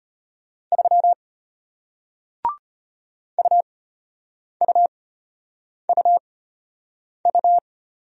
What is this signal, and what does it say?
Morse code, heard as a keyed tone, about 34 wpm, spaced as for 8 wpm, 700 Hz. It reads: 3 UUUU